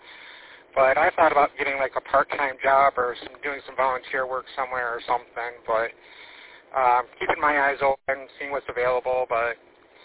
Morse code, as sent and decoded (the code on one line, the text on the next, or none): none